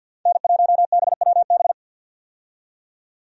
Morse code, heard as a keyed tone, about 50 wpm, 700 Hz. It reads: N1BWB
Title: USA Call Signs - 5 Characters - 50wpm